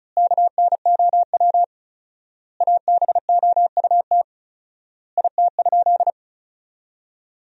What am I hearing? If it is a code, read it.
Morse code, heard as a keyed tone, about 34 wpm, 700 Hz. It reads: KNOW ABOUT IT?
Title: 4x Word-Spacing - Sentences from Top 100 Words - 35wpm